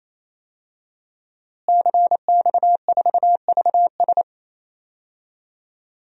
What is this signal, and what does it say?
Morse code, heard as a keyed tone, about 28 wpm, 700 Hz. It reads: CX4VS
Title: DXCC-100 Call Signs - 5 Characters - Repeated 3x - 28wpm